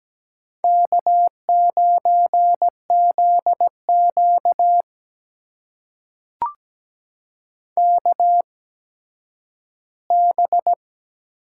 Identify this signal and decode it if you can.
Morse code, heard as a keyed tone, about 17 wpm, spaced as for 10 wpm, 700 Hz. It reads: K9ZQ KB
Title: USA Call Signs - 8x Character Spacing - 17wpm